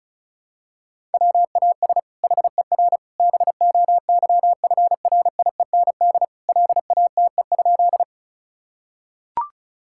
Morse code, heard as a keyed tone, about 34 wpm, 690 Hz. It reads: WAS HER BOYFRIEND LATE?
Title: Sentences from Top 1000 Words - Review Part 5 - 35wpm